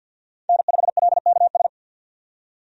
Morse code, heard as a keyed tone, about 48 wpm, 700 Hz. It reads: N5LKS
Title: USA Call Signs - 5 Characters - 50wpm